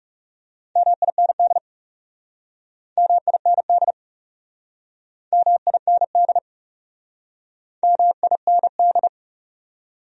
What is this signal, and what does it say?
Morse code, heard as a keyed tone, about 41 wpm, 700 Hz. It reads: MIND MIND MIND MIND